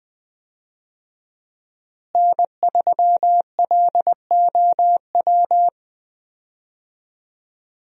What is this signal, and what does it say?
Morse code, heard as a keyed tone, about 21 wpm, 700 Hz. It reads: N3LOW
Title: USA Call Signs - 5 Characters - Repeated 3x - 20wpm